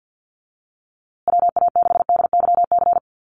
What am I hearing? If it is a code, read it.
Morse code, heard as a keyed tone, about 50 wpm, 700 Hz. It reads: WA6DYC
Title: USA Call Signs Repeated 3x 50wpm